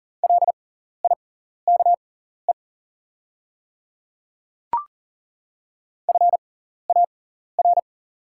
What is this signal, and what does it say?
Morse code, heard as a keyed tone, about 38 wpm, spaced as for 14 wpm, 700 Hz. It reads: LIKE FAR
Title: 6x Character Spacing - Top 300 Words - 40wpm